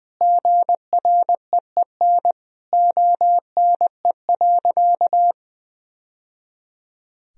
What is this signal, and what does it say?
Morse code, heard as a keyed tone, about 20 wpm, 700 Hz. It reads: GREEN ONE.
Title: Sentences from Top 1000 Words - Review Part 2 - 20wpm